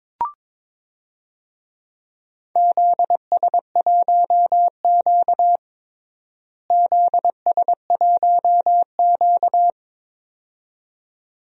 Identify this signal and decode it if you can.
Morse code, heard as a keyed tone, about 21 wpm, 700 Hz. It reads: ZS1Q ZS1Q
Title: DXCC-100 Call Signs - 4 Characters - Repeated 2x - 22wpm